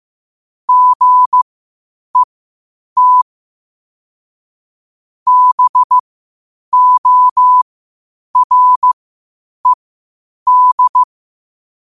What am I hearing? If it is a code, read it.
Morse code, heard as a keyed tone, about 15 wpm, spaced as for 8 wpm, 1000 Hz. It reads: GET BORED
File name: qotdfiles / 8wpm